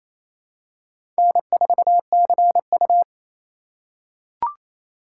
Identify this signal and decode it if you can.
Morse code, heard as a keyed tone, about 28 wpm, 700 Hz. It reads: N4CU
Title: USA Call Signs - 4 Characters - 3x Character Spacing - 28wpm